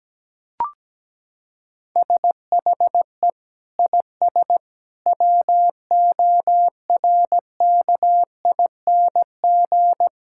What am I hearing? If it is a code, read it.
Morse code, heard as a keyed tone, about 17 wpm, 700 Hz. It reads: SHE IS WORKING
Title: Sentences from Top 1000 Words - Review Part 5 - 17wpm